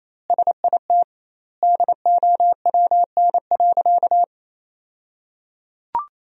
Morse code, heard as a keyed tone, about 28 wpm, 700 Hz. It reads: SIT DOWN.
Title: 2x Word-Spacing - Sentences from Top 500 Words - 28wpm